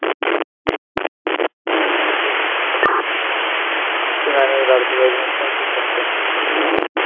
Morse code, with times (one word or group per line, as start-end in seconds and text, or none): none